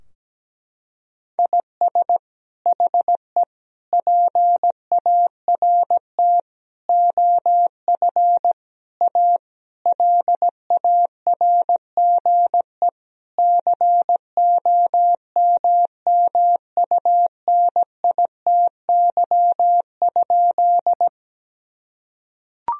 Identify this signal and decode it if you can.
Morse code, heard as a keyed tone, about 17 wpm, 700 Hz. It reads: IS HE PART OF A LARGE COMMUNITY?
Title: Sentences from Top 1000 Words - Review Part 4 - 17wpm